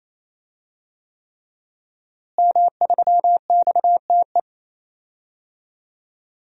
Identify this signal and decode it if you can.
Morse code, heard as a keyed tone, about 28 wpm, 700 Hz. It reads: M3XTE
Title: DXCC-100 Call Signs - 5 Characters - Repeated 3x - 28wpm